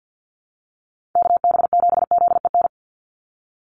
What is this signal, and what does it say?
Morse code, heard as a keyed tone, about 50 wpm, 700 Hz. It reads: KB7ZEN
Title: USA Call Signs 50wpm